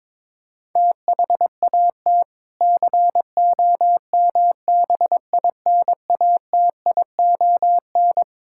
Morse code, heard as a keyed tone, about 21 wpm, 700 Hz. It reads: THAT COMBINATION
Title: Sentences from Top 1000 Words - Review Part 6 - 22wpm